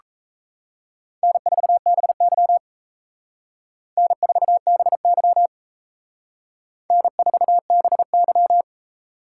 Morse code, heard as a keyed tone, about 34 wpm, 700 Hz. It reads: N4BY N4BY N4BY
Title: Speed Racing - USA 1x2 Call Signs 28wpm